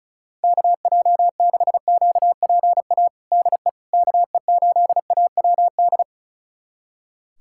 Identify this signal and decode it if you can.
Morse code, heard as a keyed tone, about 36 wpm, 690 Hz. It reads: KJ6QPA DE KE8AWD
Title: USA Callsign DE Callsign - 6 Characters - 35wpm